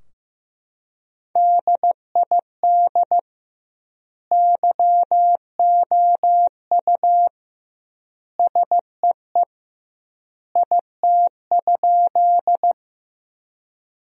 Morse code, heard as a keyed tone, about 15 wpm, 700 Hz. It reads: DID YOU SEE IT?
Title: 2x Word-Spacing - Sentences from Top 100 Words - 15wpm